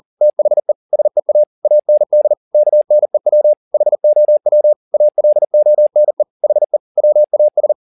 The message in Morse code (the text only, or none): THE SEA AND KNEW HOW ALONE HE WAS